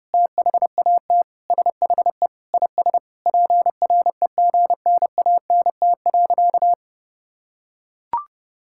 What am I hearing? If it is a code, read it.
Morse code, heard as a keyed tone, about 30 wpm, 700 Hz. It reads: THAT SHE IS PREGNANT.
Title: Sentences from Top 1000 Words - Review Part 6 - 30wpm